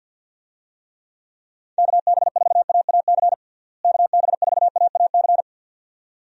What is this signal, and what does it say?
Morse code, heard as a keyed tone, about 52 wpm, 700 Hz. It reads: KB4AAC KB4AAC
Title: USA Call Signs - 6 Characters - Repeated 2x - 50wpm